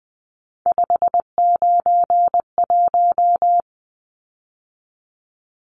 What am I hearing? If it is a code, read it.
Morse code, heard as a keyed tone, about 20 wpm, 700 Hz. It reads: 591
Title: Numbers - 3-Digit 20wpm